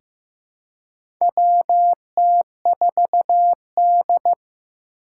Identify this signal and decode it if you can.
Morse code, heard as a keyed tone, about 15 wpm, 700 Hz. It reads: WT4D
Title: USA Call Signs - 4 Characters - 2x Character Spacing - 15wpm